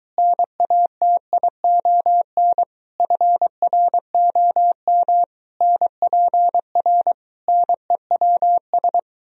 Morse code, heard as a keyed tone, about 23 wpm, 700 Hz. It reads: NATION FROM NPR NEWS